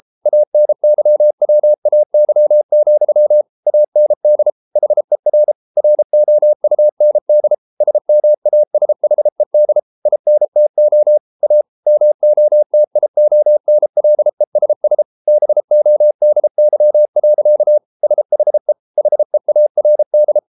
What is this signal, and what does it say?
Morse code, heard as a keyed tone, about 33 wpm, 600 Hz. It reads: ANYWAY, AND HER ROUND SMASHED INTO A MOTIONLESS BODY. SHE HEARD